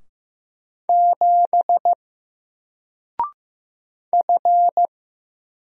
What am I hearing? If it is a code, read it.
Morse code, heard as a keyed tone, about 15 wpm, 700 Hz. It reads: 7 F
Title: Single Letter-Number 15wpm